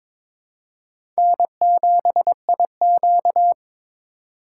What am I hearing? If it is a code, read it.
Morse code, heard as a keyed tone, about 21 wpm, 700 Hz. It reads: N7IQ
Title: USA Call Signs - 2x Character Spacing - 22wpm